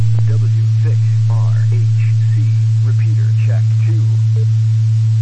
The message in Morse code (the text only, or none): none